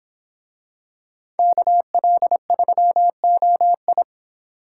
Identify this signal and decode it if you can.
Morse code, heard as a keyed tone, about 25 wpm, 700 Hz. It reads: KL3OI